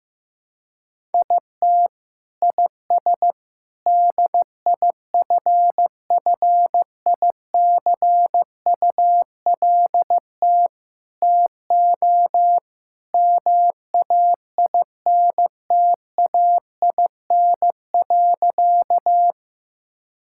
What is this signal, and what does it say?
Morse code, heard as a keyed tone, about 16 wpm, 700 Hz. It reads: IT IS DIFFICULT TO MAINTAIN.
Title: Sentences from Top 1000 Words - Review Part 5 - 15wpm